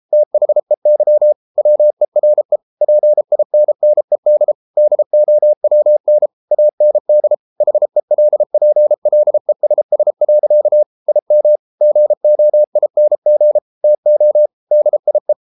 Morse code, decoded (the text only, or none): THEY WERE PINNED DOWN AND HELPLESS. IM GOING TO DIE